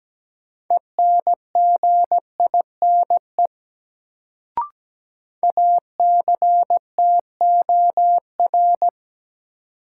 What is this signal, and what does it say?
Morse code, heard as a keyed tone, about 17 wpm, 700 Hz. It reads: ENGINE ACTOR
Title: Sentences from Top 1500 Words - New Word Drill - 17wpm